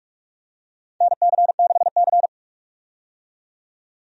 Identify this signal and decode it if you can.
Morse code, heard as a keyed tone, about 47 wpm, 700 Hz. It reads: NC6C
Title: USA Call Signs - 4 Characters - 45wpm